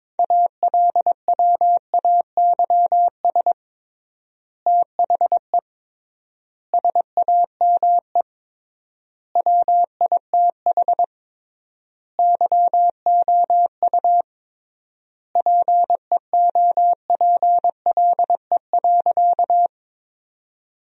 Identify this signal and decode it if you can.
Morse code, heard as a keyed tone, about 22 wpm, 700 Hz. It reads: ALWAYS THE SAME WITH YOU PEOPLE.
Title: 3x Word-Spacing - Sentences from Top 300 Words - 22wpm